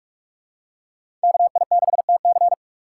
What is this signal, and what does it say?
Morse code, heard as a keyed tone, about 42 wpm, 700 Hz. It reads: KI6TC
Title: USA Call Signs - 5 Characters - 3x Character Spacing - 45wpm